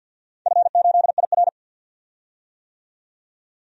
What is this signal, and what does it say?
Morse code, heard as a keyed tone, about 45 wpm, 700 Hz. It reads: F8IR
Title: DXCC-100 Call Signs - 4 Characters - 50wpm